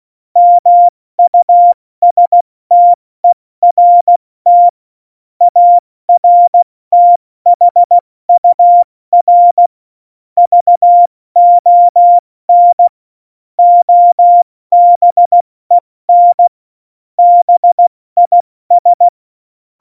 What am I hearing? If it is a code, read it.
Morse code, heard as a keyed tone, about 16 wpm, 700 Hz. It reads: MUSTERT ARTHUR VON OBEN BIS
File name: Morsekurs / Witze / Witz 26. Januar 2021, 70 F80